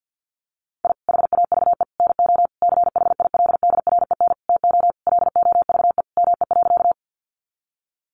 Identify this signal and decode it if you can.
Morse code, heard as a keyed tone, about 50 wpm, 700 Hz. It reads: I HAVE NO CHILDREN TO LOVE ME.
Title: Sentences from Top 400 Words 50wpm